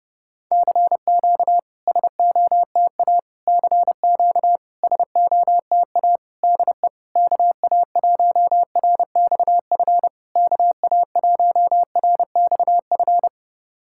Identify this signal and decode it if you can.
Morse code, heard as a keyed tone, about 30 wpm, 700 Hz. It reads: CQ SOTA CQ SOTA DE KA1RXF KA1RXF
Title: SOTA QSO Practice 30wpm